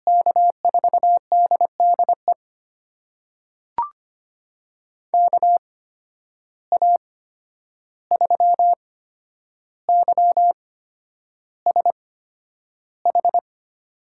Morse code, heard as a keyed tone, about 25 wpm, spaced as for 10 wpm, 700 Hz. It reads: K4DDE KA3YSH
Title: USA Call Signs - 6 Characters - 8x Character Spacing - 25wpm